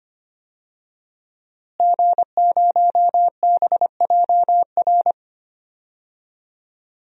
Morse code, heard as a keyed tone, about 25 wpm, 700 Hz. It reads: G0BJR
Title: DXCC-100 Call Signs - 5 Characters - Repeated 2x - 25wpm